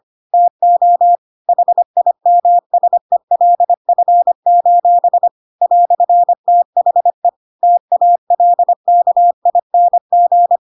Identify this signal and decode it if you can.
Morse code, heard as a keyed tone, about 25 wpm, 700 Hz. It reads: TO HIMSELF: "THE TALKING
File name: Pinocchio / Pinocchio ch5 25 25 ik1hge